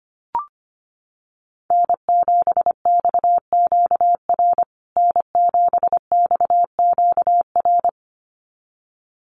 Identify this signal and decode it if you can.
Morse code, heard as a keyed tone, about 25 wpm, 700 Hz. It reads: N7XQR N7XQR
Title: USA Call Signs Repeated 2x 25wpm